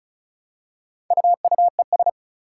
Morse code, heard as a keyed tone, about 36 wpm, 700 Hz. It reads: UUES
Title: Four Challenging Letter-Number Combinations 35wpm